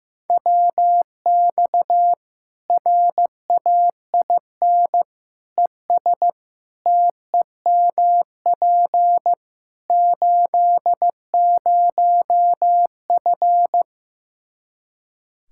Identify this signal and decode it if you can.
Morse code, heard as a keyed tone, about 16 wpm, 700 Hz. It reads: WX RAIN ES TEMP 80F
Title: K1USN SST QSO Practice 15wpm